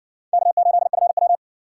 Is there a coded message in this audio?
Yes: Morse code from 0.3 to 1.4 s, heard as a keyed tone, about 60 wpm, 700 Hz. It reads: K8FW